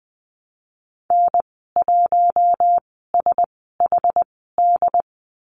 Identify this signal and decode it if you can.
Morse code, heard as a keyed tone, about 20 wpm, spaced as for 15 wpm, 700 Hz. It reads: N1SHD
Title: 193 Lesson 31g - Letter v - Call Signs 20wpm